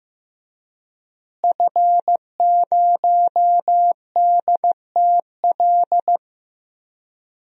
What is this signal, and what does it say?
Morse code, heard as a keyed tone, about 16 wpm, 700 Hz. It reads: F0DTL
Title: DXCC-100 Call Signs - 5 Characters - Repeated 3x - 15wpm